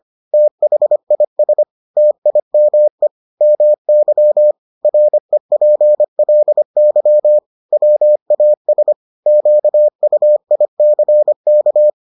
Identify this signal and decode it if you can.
Morse code, heard as a keyed tone, about 25 wpm, 600 Hz. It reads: THIS TIME MY REPLY WAS QUICK